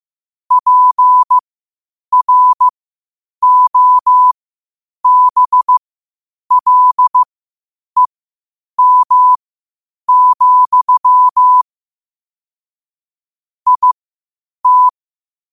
Morse code, heard as a keyed tone, about 15 wpm, spaced as for 8 wpm, 1000 Hz. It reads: PROBLEM, IT
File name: qotdfiles / 8wpm